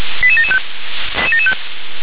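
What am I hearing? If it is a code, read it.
Five-tone selective call (ZVEI), heard as quick runs of five tones, tones of about 70 ms. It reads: 88775
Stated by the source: ZVEI five-tone call 88774 received via SDR.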